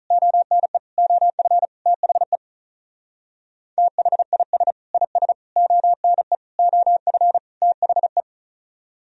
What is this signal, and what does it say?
Morse code, heard as a keyed tone, about 43 wpm, 700 Hz. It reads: ONE OF THE THIS IS ONE OF THE
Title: Speed-Racing - Sets of 5 Words 35wpm